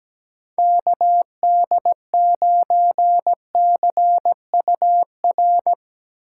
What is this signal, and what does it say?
Morse code, heard as a keyed tone, about 19 wpm, 700 Hz. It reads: KD9CUR
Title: Speed Racing - USA 2x3 Call Signs 17wpm